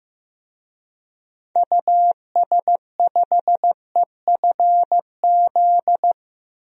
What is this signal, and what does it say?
Morse code, heard as a keyed tone, about 15 wpm, 700 Hz. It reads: US5EFZ
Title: DXCC-100 Call Signs - 6 Characters - 15wpm